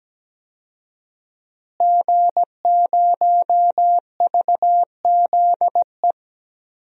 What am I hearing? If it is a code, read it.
Morse code, heard as a keyed tone, about 17 wpm, 700 Hz. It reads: G0VZE
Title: DXCC-100 Call Signs - 5 Characters - Repeated 3x - 17wpm